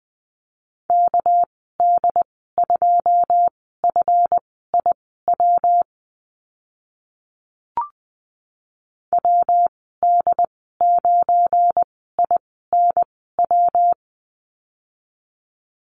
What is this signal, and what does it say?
Morse code, heard as a keyed tone, about 20 wpm, spaced as for 14 wpm, 700 Hz. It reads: KD2FIW WD9INW
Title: 207 Lesson 33g - Letter j - Call Signs 20wpm